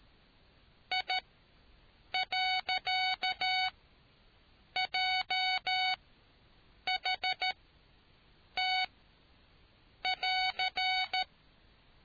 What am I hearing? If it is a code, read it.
Morse code, heard as a keyed tone, about 13 wpm, spaced as for 6 wpm, 750 Hz. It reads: I.JHT<AR>